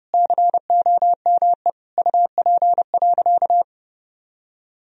Morse code, heard as a keyed tone, about 30 wpm, 700 Hz. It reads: COME UP.
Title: Sentences from Top 1000 Words - Review Part 2 - 30wpm